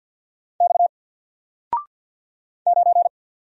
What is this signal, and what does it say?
Morse code, heard as a keyed tone, about 48 wpm, 700 Hz. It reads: X 9
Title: Single Letter-Number Alt 50wpm